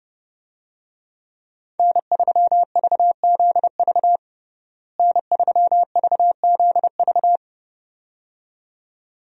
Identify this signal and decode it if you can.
Morse code, heard as a keyed tone, about 32 wpm, 700 Hz. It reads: N3VZV N3VZV
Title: USA Call Signs - 5 Characters - Repeated 2x - 30wpm